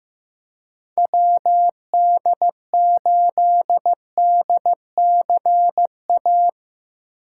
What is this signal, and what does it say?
Morse code, heard as a keyed tone, about 15 wpm, 700 Hz. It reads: WD8DCA
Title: USA Call Signs - 6 Characters - 3x Character Spacing - 15wpm